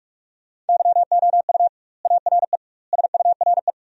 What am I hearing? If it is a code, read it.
Morse code, heard as a keyed tone, about 46 wpm, 700 Hz. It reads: YOU ARE SURE
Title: 2x Word-Spacing - Sentences from Top 300 Words - 45wpm